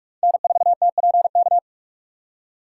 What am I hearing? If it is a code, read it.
Morse code, heard as a keyed tone, about 46 wpm, 700 Hz. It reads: N4TPK
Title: USA Call Signs - 3x Character Spacing - 45wpm